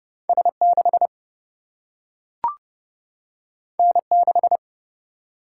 Morse code, heard as a keyed tone, about 30 wpm, 700 Hz. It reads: S 6 N 6
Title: Single-Letter Single-Number 30wpm